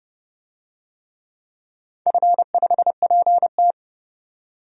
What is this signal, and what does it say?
Morse code, heard as a keyed tone, about 30 wpm, 700 Hz. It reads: F5PT
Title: DXCC-100 Call Signs - 4 Characters - 30wpm